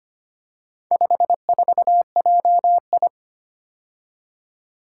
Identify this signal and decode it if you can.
Morse code, heard as a keyed tone, about 25 wpm, 700 Hz. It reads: 54JI
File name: practice / 25wpm / Four Letter-Number Combinations - 25wpm